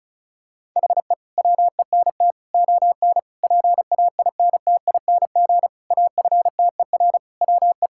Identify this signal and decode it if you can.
Morse code, heard as a keyed tone, about 36 wpm, 700 Hz. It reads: HE WENT ON PAINTING AFTER WE
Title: Sentences from Top 1000 Words - Review Part 6 - 35wpm